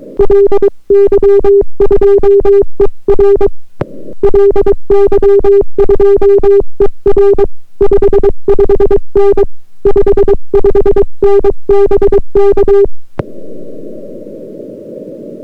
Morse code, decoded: LY2ER LY2ER 55N 55N BK